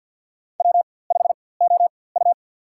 Morse code, heard as a keyed tone, about 52 wpm, spaced as for 23 wpm, 700 Hz. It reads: W5OU
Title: USA Call Signs - 4x Character Spacing - 50wpm